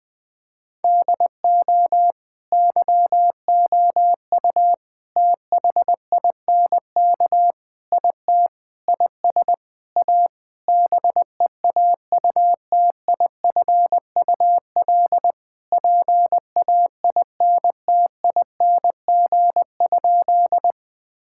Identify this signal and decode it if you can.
Morse code, heard as a keyed tone, about 21 wpm, 700 Hz. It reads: DO YOU THINK IT IS A BEAUTIFUL PAINTING?
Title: Sentences from Top 1000 Words - Review Part 6 - 20wpm